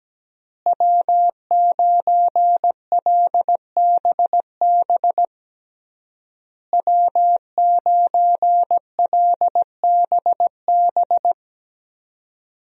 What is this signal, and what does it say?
Morse code, heard as a keyed tone, about 17 wpm, 700 Hz. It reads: W9LBB W9LBB
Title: USA Call Signs - 5 Characters - Repeated 3x - 17wpm